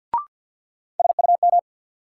Morse code, heard as a keyed tone, about 48 wpm, 700 Hz. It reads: SUM
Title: Sentences from Top 1000 Words - Review Part 6 - New Word Drill - 50wpm